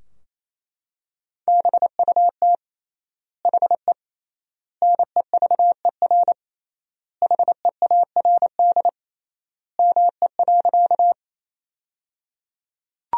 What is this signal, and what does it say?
Morse code, heard as a keyed tone, about 28 wpm, 700 Hz. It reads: BUT HE NEVER HEARD ME.
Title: 3x Word-Spacing - Sentences from Top 400 Words - 28wpm